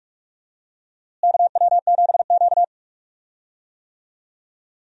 Morse code, heard as a keyed tone, about 44 wpm, 700 Hz. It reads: KW7Q
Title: USA Call Signs - 4 Characters - Repeated 2x - 45wpm